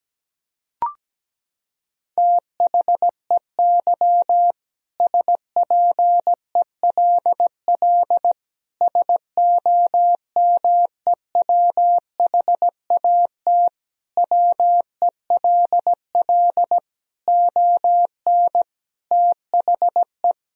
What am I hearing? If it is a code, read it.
Morse code, heard as a keyed tone, about 17 wpm, 700 Hz. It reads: THEY SPELL SOMEWHAT WELL ON THE
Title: Sentences from Top 1000 Words - Review Part 1 - 17wpm